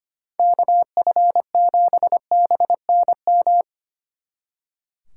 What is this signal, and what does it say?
Morse code, heard as a keyed tone, about 29 wpm, 700 Hz. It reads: KF7BNM
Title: Speed Racing - USA 2x3 Call Signs 25wpm